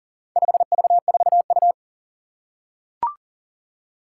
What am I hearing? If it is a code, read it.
Morse code, heard as a keyed tone, about 40 wpm, 700 Hz. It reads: 5V4U
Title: Four Challenging Letter-Number Combinations 40wpm